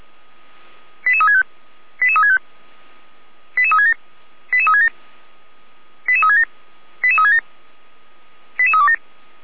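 Five-tone selective call (ZVEI), heard as quick runs of five tones, tones of about 70 ms. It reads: 88365 88365 88367 88367 88367 88367 88327
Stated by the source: Received by SDR, ZVEI five-tone selective call.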